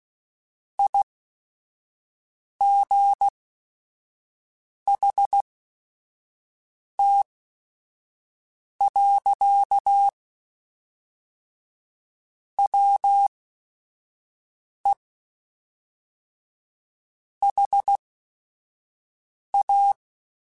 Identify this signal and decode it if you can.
Morse code, heard as a keyed tone, about 16 wpm, spaced as for 5 wpm, 800 Hz. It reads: IGHT. WE HA
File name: QSOs / QSOs-mp3s / WD0BGP 1@5